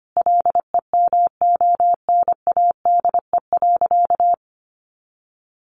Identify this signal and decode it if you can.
Morse code, heard as a keyed tone, about 25 wpm, 700 Hz. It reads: LEMONADE.
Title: Additional Easy Phrases 25wpm